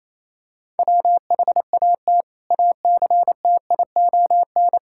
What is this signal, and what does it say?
Morse code, heard as a keyed tone, about 28 wpm, 700 Hz. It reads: WHAT ACTION
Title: Sentences from Top 1000 Words - Review Part 6 - 28wpm